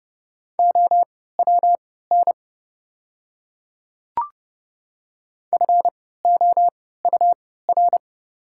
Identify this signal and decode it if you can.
Morse code, heard as a keyed tone, about 30 wpm, spaced as for 17 wpm, 690 Hz. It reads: OWN FOUR
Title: 3x Character Spacing - Top 500 Words - 30wpm